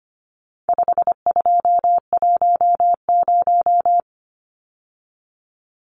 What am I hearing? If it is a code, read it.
Morse code, heard as a keyed tone, about 25 wpm, 700 Hz. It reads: 5210
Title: Numbers - 4-Digit 25wpm